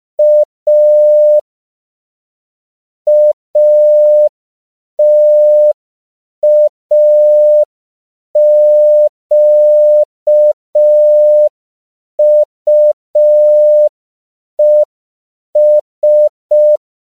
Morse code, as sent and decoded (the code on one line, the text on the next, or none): .- / .- - .- --.- ..- . ...
A ATAQUES